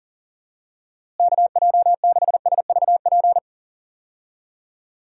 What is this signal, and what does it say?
Morse code, heard as a keyed tone, about 40 wpm, 700 Hz. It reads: KJ6SVP
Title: USA Call Signs - 6 Characters - Repeated 2x - 40wpm